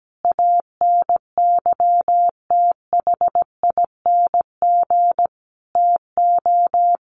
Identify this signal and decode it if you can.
Morse code, heard as a keyed tone, about 17 wpm, 700 Hz. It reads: ANYTHING TO
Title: Sentences from Top 1100 Words 17wpm